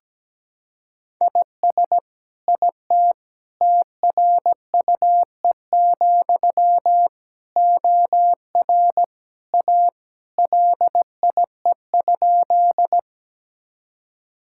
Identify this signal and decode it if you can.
Morse code, heard as a keyed tone, about 17 wpm, 700 Hz. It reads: IS IT TRUE, OR A LIE?
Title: Sentences from Top 1600 Words 17wpm